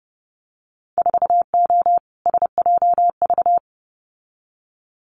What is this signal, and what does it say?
Morse code, heard as a keyed tone, about 30 wpm, 700 Hz. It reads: 4O SJV
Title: Winter Field Day Exchanges Encore 30wpm